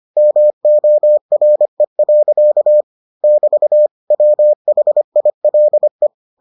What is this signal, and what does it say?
Morse code, heard as a keyed tone, about 25 wpm, 600 Hz. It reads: MORE. <BT> WHILE